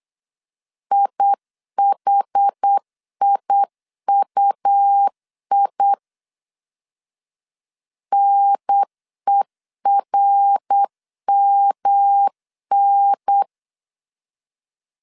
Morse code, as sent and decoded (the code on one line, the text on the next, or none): .. .... .. ..- .. / -. .-. -- -.
IHIUI NERMN